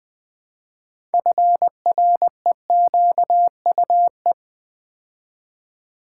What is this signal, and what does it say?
Morse code, heard as a keyed tone, about 20 wpm, 700 Hz. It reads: FREQUE
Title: Build A Word - 2 Letters At A Time - 8-Letter Words - 20wpm